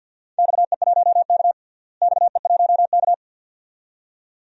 Morse code, heard as a keyed tone, about 50 wpm, 700 Hz. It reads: XE1K XE1K
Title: DXCC-100 Call Signs - 4 Characters - Repeated 2x - 50wpm